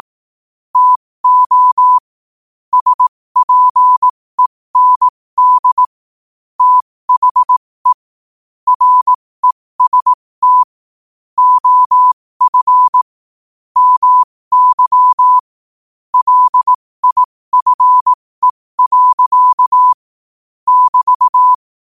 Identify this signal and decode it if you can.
Morse code, heard as a keyed tone, about 18 wpm, spaced as for 15 wpm, 1000 Hz. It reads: TO SPEND THE REST OF MY LIFE. <BT>
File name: qotdfiles / 15wpm